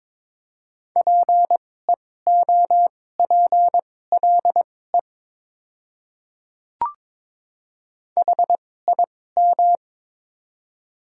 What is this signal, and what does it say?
Morse code, heard as a keyed tone, about 22 wpm, spaced as for 16 wpm, 700 Hz. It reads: PEOPLE HIM
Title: 2x Character Spacing - Top 100 Words - 22wpm